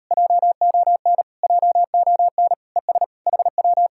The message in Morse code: JON JON ES HW

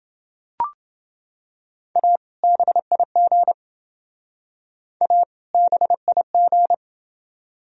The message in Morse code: A BIG A BIG

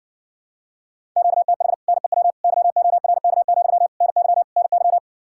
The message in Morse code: QTH NEW YORK, NY NY

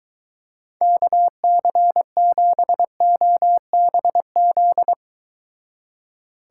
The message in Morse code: KC7OBZ

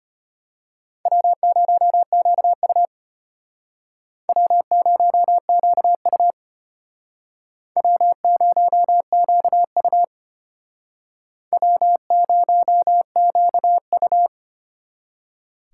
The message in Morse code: W0QU W0QU W0QU W0QU

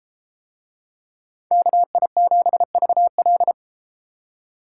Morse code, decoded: KI7VL

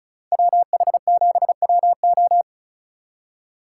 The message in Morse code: WH7WO

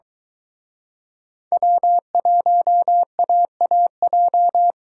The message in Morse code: W1AAJ

